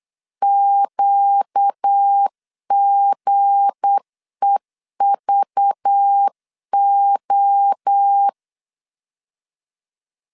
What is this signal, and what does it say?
Morse code, heard as a keyed tone, about 8 wpm, 800 Hz. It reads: QGEVO
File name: LEZIONI / 7 GF9Z / 7 G 8wpm 3118